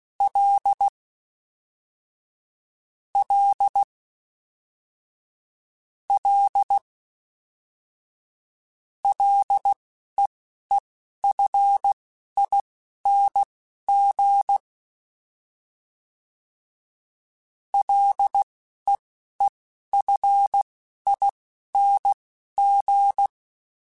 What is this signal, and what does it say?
Morse code, heard as a keyed tone, about 16 wpm, spaced as for 12 wpm, 800 Hz. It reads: L L L LEEFING LEEFING
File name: Ls / Single L m3s / Single L Prefixe 3x2 Twice @10